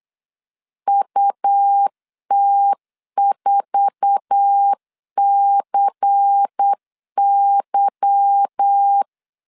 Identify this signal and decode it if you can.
Morse code, heard as a keyed tone, about 8 wpm, 800 Hz. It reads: UT4CY
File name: CW / DET / 1 9 1500C S C 8wpm 5653